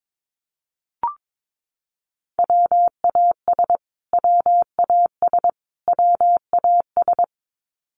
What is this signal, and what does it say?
Morse code, heard as a keyed tone, about 22 wpm, 700 Hz. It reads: WAS WAS WAS
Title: Top 500 Words Repeated 22wpm